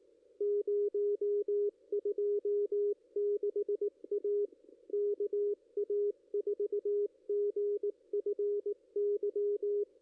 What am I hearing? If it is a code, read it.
Morse code, heard as a keyed tone, about 18 wpm, 440 Hz, with radio noise. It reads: 026 A KA4GFY
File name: contests / ss-cw-k5tr-op-n5rz -2300